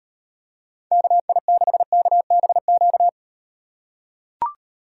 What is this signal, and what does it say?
Morse code, heard as a keyed tone, about 38 wpm, 700 Hz. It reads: KI6KBQ